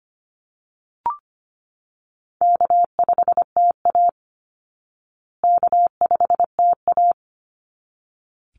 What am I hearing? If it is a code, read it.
Morse code, heard as a keyed tone, about 25 wpm, 700 Hz. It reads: K5TA K5TA
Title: Easy Call Signs Repeated 2x - 25wpm